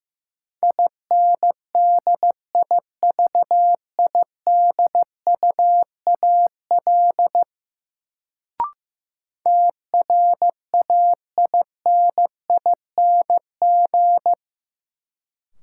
Morse code, heard as a keyed tone, about 15 wpm, 700 Hz. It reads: INDIVIDUAL TRAINING
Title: Sentences from Top 1400 Words - New Word Drill Alt - 15wpm